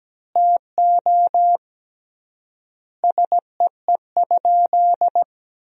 Morse code, heard as a keyed tone, about 17 wpm, 700 Hz. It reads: TO SEE?
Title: 3x Word-Spacing - Sentences from Top 100 Words - 17wpm